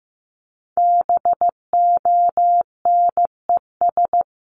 Morse code, heard as a keyed tone, about 15 wpm, 700 Hz. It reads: BONES
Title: A Sound Of Thunder 15wpm